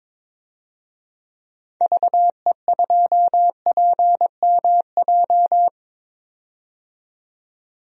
Morse code, heard as a keyed tone, about 22 wpm, 700 Hz. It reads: VE2PMJ
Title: DXCC-100 Call Signs - 6 Characters - 22wpm